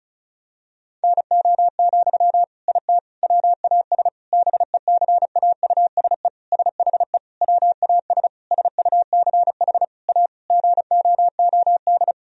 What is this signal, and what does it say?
Morse code, heard as a keyed tone, about 36 wpm, 700 Hz. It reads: NO, IT WAS BECAUSE SHE WAS SUCH A GOOD
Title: Sentences from Top 1000 Words - Review Part 4 - 35wpm